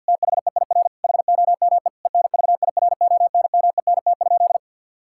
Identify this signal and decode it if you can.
Morse code, heard as a keyed tone, about 51 wpm, 700 Hz. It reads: THEIR HOME ENVIRONMENT?
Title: Sentences from Top 1000 Words - Review Part 5 - 50wpm